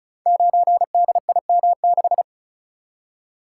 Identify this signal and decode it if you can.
Morse code, heard as a keyed tone, about 36 wpm, 700 Hz. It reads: 9DIM6
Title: Code Groups - 5 Letter:Numbers - 35wpm